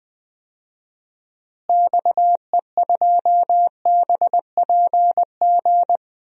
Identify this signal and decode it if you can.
Morse code, heard as a keyed tone, about 21 wpm, 700 Hz. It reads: XE2BPG
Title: DXCC-100 Call Signs - 6 Characters - Repeated 3x - 20wpm